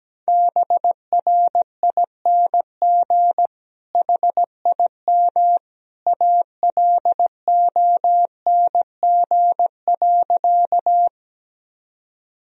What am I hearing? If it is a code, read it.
Morse code, heard as a keyed tone, about 17 wpm, 700 Hz. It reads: BRING HIM ALONG.
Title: Sentences from Top 1000 Words - Review Part 3 - 17wpm